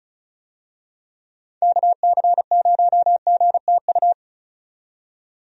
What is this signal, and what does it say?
Morse code, heard as a keyed tone, about 35 wpm, 700 Hz. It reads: KC0GTU
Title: USA Call Signs - 6 Characters - 35wpm